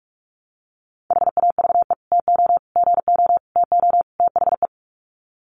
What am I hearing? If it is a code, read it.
Morse code, heard as a keyed tone, about 44 wpm, 700 Hz. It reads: HAVE TO GO TO THE
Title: Sets of 5 Words Encore 45wpm